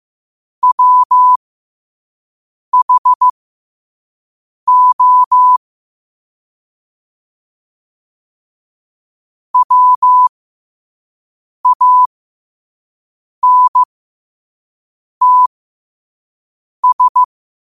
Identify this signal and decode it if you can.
Morse code, heard as a keyed tone, about 15 wpm, spaced as for 5 wpm, 1000 Hz. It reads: WHO WANTS